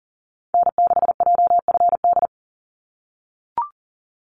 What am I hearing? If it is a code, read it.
Morse code, heard as a keyed tone, about 40 wpm, 700 Hz. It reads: N6JFD
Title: NA SOTA Activator Callsigns 40wpm